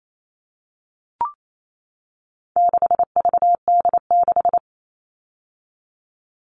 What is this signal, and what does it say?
Morse code, heard as a keyed tone, about 28 wpm, 700 Hz. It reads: 6VD6
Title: All Cut Numbers - 4-Digit 28wpm